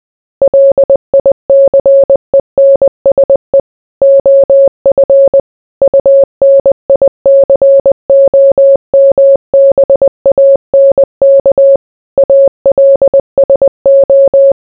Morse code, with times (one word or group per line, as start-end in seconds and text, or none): 0.4-3.6 LICENSE
4.0-5.4 OF
5.8-11.8 UNICOMBANK
12.2-14.5 ALSO